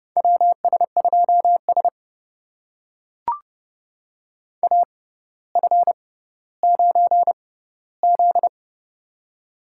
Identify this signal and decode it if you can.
Morse code, heard as a keyed tone, about 30 wpm, spaced as for 15 wpm, 700 Hz. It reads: WS2S AF9Z
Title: USA Call Signs - 4 Characters - 6x Character Spacing - 30wpm